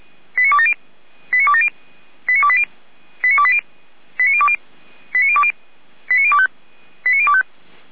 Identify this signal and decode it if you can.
Five-tone selective call (ZVEI), heard as quick runs of five tones, tones of about 70 ms. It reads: 89270 89270 89280 89280 89020 89020 89025 89025